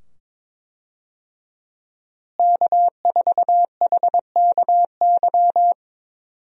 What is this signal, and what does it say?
Morse code, heard as a keyed tone, about 22 wpm, 700 Hz. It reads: K4HKY
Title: USA Call Signs - 5 Characters - Repeated 2x - 22wpm